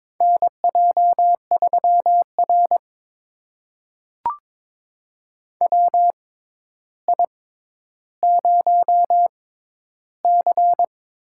Morse code, heard as a keyed tone, about 21 wpm, spaced as for 11 wpm, 700 Hz. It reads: NJ3R WI0C
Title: USA Call Signs - 4 Characters - 6x Character Spacing - 22wpm